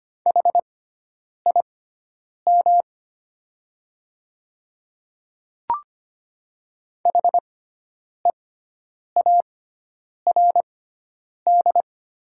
Morse code, heard as a keyed tone, about 25 wpm, spaced as for 9 wpm, 700 Hz. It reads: HIM HEARD
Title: 6x Character Spacing - Top 500 Words - 25wpm